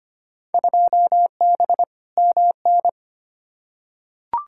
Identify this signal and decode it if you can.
Morse code, heard as a keyed tone, about 25 wpm, 700 Hz. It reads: 2B MN
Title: Field Day Exchanges - 25wpm